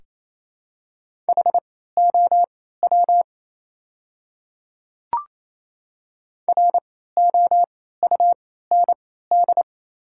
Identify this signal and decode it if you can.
Morse code, heard as a keyed tone, about 28 wpm, spaced as for 16 wpm, 700 Hz. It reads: HOW ROUND
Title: 3x Character Spacing - Top 200 Words - 28wpm